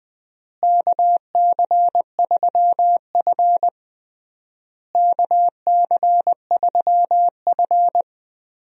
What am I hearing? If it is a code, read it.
Morse code, heard as a keyed tone, about 20 wpm, 700 Hz. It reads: KC3F KC3F
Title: USA Call Signs - 4 Characters - Repeated 3x - 20wpm